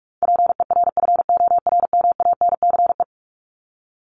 Morse code, heard as a keyed tone, about 44 wpm, 700 Hz. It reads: PERFORMANCE
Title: Top 751-1000 Words 45wpm